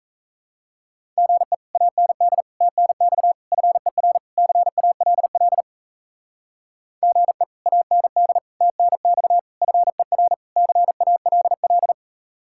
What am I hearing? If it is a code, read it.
Morse code, heard as a keyed tone, about 36 wpm, 700 Hz. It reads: GE AND TNX FER CALL GE AND TNX FER CALL